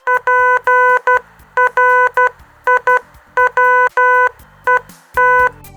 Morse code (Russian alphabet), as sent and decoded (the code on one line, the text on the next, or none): .--. .-. .. .-- . -
ПРИВЕТ